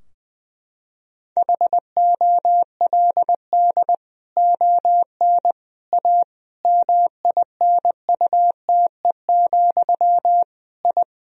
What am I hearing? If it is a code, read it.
Morse code, heard as a keyed tone, about 20 wpm, 700 Hz. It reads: HOLD ON A MINUTE, I
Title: Sentences from Top 1000 Words - Review Part 2 - 20wpm